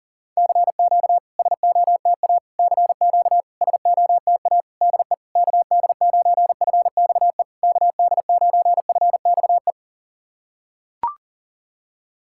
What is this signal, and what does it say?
Morse code, heard as a keyed tone, about 40 wpm, 700 Hz. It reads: CQ SOTA CQ SOTA DE KD9FXE KD9FXE
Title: SOTA QSO Practice 40wpm